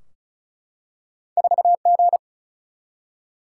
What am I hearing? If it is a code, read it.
Morse code, heard as a keyed tone, about 36 wpm, 700 Hz. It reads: 4 G
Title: Single-Letter Single-Number 35wpm